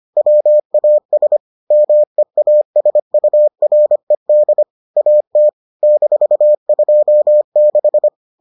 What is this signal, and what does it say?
Morse code, heard as a keyed tone, about 25 wpm, 600 Hz. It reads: WAS MEASURED AT -26